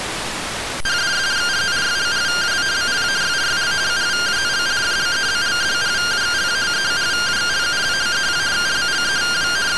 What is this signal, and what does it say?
Two-tone teleprinter signal, about 170 Hz shift.
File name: findings / MR AXIAL 14.gen-08 rtty rx